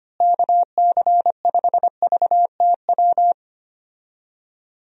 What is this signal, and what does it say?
Morse code, heard as a keyed tone, about 25 wpm, 700 Hz. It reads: KC5VTW
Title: USA Call Signs - 6 Characters - Repeated 2x - 25wpm